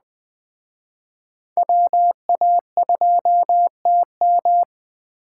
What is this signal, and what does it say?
Morse code, heard as a keyed tone, about 21 wpm, 700 Hz. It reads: WA2TM